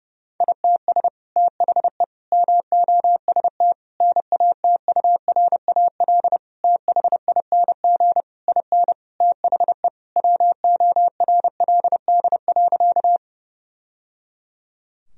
Morse code, heard as a keyed tone, about 30 wpm, 700 Hz. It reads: ITS THE MOST NATURAL THING IN THE WORLD.
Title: Sentences from Top 1000 Words - Review Part 4 - 30wpm